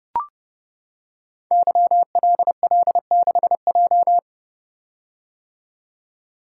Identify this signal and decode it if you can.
Morse code, heard as a keyed tone, about 29 wpm, 700 Hz. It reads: YLL6J
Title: Code Groups - 5 Letter:Numbers - 30wpm